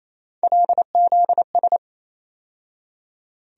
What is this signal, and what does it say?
Morse code, heard as a keyed tone, about 28 wpm, 700 Hz. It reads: LZS
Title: USA Call Sign Postfixes - 3 Letters 28wpm